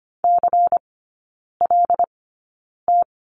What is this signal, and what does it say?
Morse code, heard as a keyed tone, about 25 wpm, spaced as for 9 wpm, 700 Hz. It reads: CLT
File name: course / 25:10wpm / 093 Lesson 16e - Letter c - Three Characters 25wpm